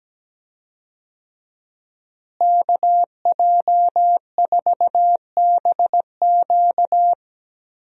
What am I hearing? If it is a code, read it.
Morse code, heard as a keyed tone, about 17 wpm, 700 Hz. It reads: KJ4BQ
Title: USA Call Signs - 5 Characters - Repeated 2x - 17wpm